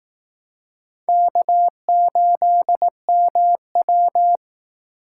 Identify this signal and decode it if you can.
Morse code, heard as a keyed tone, about 18 wpm, 700 Hz. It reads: K8MW
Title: Speed Racing - USA Call Signs 15wpm